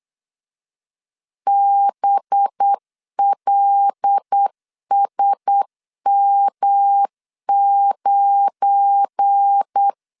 Morse code, heard as a keyed tone, about 8 wpm, 800 Hz. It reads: BLSM9
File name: LEZIONI / 7 GF9Z / 7 GF9Z 8wpm 3213